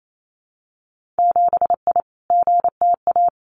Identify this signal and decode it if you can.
Morse code, heard as a keyed tone, about 28 wpm, 700 Hz. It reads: 7I GTA
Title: Winter Field Day Exchanges Encore 28wpm